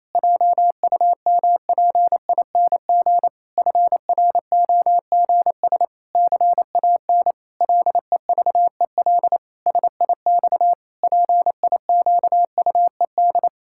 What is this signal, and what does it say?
Morse code, heard as a keyed tone, about 28 wpm, 700 Hz. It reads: JUMPING FROGS CAN LEVEL SIX PIQUED
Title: Pangram Challenge v2 - 28wpm